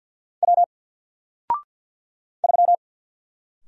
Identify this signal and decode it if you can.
Morse code, heard as a keyed tone, about 54 wpm, 700 Hz. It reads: W 3
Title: Single Letter-Number Alt 50wpm